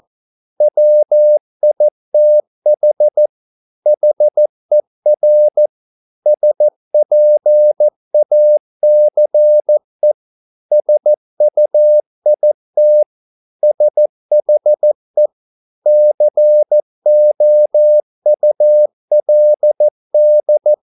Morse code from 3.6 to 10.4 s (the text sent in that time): HER SPACE